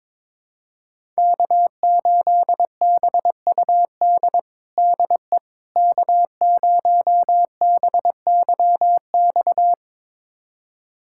Morse code, heard as a keyed tone, about 22 wpm, 700 Hz. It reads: K8BUD DE K0BYX
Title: USA Callsign DE Callsign - 5 Characters - 22wpm